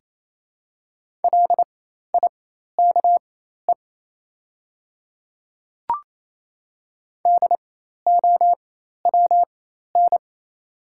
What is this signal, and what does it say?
Morse code, heard as a keyed tone, about 28 wpm, spaced as for 13 wpm, 700 Hz. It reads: LIKE DOWN
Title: 4x Character Spacing - Top 100 Words - 28wpm